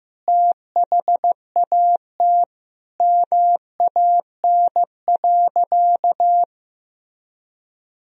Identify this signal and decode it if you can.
Morse code, heard as a keyed tone, about 16 wpm, 700 Hz. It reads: THAT MAN.
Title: Sentences from Top 1000 Words - Review Part 1 - 15wpm